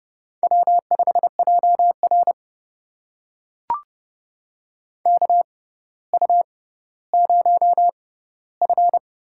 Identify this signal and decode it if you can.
Morse code, heard as a keyed tone, about 30 wpm, spaced as for 15 wpm, 700 Hz. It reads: W5JR KU0F
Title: USA Call Signs - 4 Characters - 6x Character Spacing - 30wpm